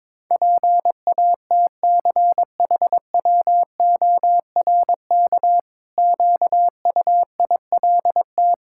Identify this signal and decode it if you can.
Morse code, heard as a keyed tone, about 22 wpm, 700 Hz. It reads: PATCHWORK QUILT
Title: Pangram Challenge v2 - 22wpm